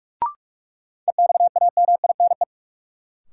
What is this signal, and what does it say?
Morse code, heard as a keyed tone, about 45 wpm, 700 Hz. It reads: EXAMINE
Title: Sentences from Top 1600 Words - New Word Drill Alt - 45wpm